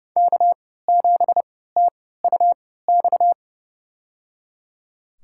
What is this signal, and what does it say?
Morse code, heard as a keyed tone, about 30 wpm, spaced as for 17 wpm, 700 Hz. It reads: K7TUX
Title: USA Call Signs - 5 Characters - 3x Character Spacing - 30wpm